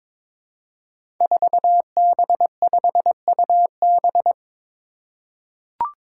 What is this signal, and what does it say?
Morse code, heard as a keyed tone, about 22 wpm, 700 Hz. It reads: 4B5UB
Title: Five Challenging Letter-Number Combinations 22wpm